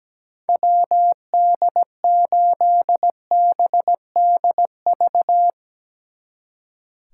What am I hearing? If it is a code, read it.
Morse code, heard as a keyed tone, about 18 wpm, 700 Hz. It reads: WD8BDV
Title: Speed Racing - USA 2x3 Call Signs 17wpm